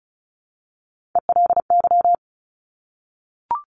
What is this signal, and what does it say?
Morse code, heard as a keyed tone, about 36 wpm, 700 Hz. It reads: ELY
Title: Top Three Letter Combinations in Words 35wpm